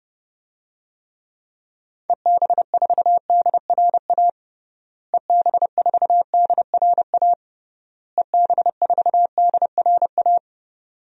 Morse code, heard as a keyed tone, about 30 wpm, 700 Hz. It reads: EB4DRA EB4DRA EB4DRA
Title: DXCC-100 Call Signs - 6 Characters - Repeated 3x - 30wpm